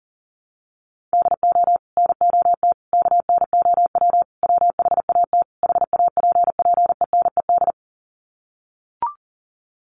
Morse code, heard as a keyed tone, about 40 wpm, 700 Hz. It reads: DO NOT KNOW WHAT HAPPENED
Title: Sets of 5 Words Encore2 40wpm